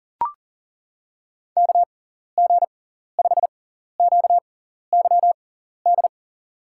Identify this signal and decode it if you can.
Morse code, heard as a keyed tone, about 40 wpm, spaced as for 14 wpm, 700 Hz. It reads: KG5QYD
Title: USA Call Signs - 6 Characters - 6x Character Spacing - 40wpm